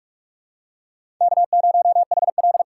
Morse code, heard as a keyed tone, about 44 wpm, 700 Hz. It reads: K0HL
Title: USA Call Signs - 4 Characters - 6x Character Spacing - 45wpm